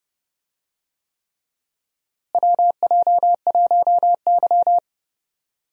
Morse code, heard as a keyed tone, about 30 wpm, 700 Hz. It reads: WJ1Y
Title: USA Call Signs - 4 Characters - Repeated 3x - 30wpm